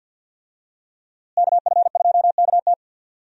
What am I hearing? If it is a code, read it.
Morse code, heard as a keyed tone, about 46 wpm, 700 Hz. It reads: KF2KT
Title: USA Call Signs - 5 Characters - Repeated 2x - 50wpm